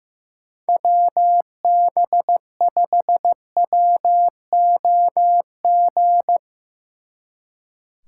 Morse code, heard as a keyed tone, about 16 wpm, 700 Hz. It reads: WB5WOG
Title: Speed Racing - USA 2x3 Call Signs 15wpm